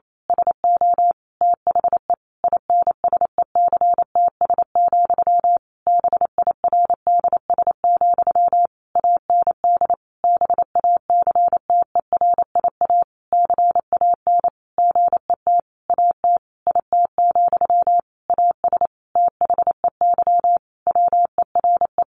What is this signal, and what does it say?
Morse code, heard as a keyed tone, about 28 wpm, 700 Hz. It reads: SO THE INSECTS, BIRDS, AND BACTERIA CAN GET AT IT, AS THEY WERE
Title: A Sound Of Thunder 28wpm